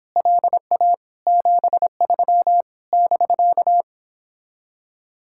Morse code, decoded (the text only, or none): LA 73 <BK>